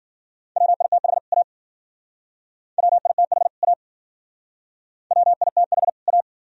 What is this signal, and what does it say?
Morse code, heard as a keyed tone, about 53 wpm, 700 Hz. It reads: WITH A WITH A WITH A